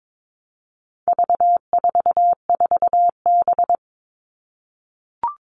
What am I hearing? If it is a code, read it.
Morse code, heard as a keyed tone, about 22 wpm, 700 Hz. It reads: V44B
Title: All Cut Numbers - 4-Digit 22wpm